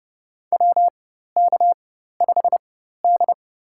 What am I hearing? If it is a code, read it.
Morse code, heard as a keyed tone, about 30 wpm, spaced as for 14 wpm, 700 Hz. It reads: WK5D
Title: USA Call Signs - 4 Characters - 4x Character Spacing - 30wpm